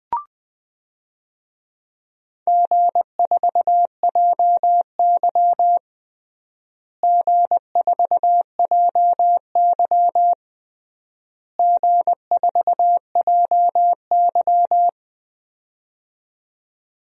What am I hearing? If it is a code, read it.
Morse code, heard as a keyed tone, about 21 wpm, 700 Hz. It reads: G4JY G4JY G4JY
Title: DXCC-100 Call Signs - 4 Characters - Repeated 3x - 20wpm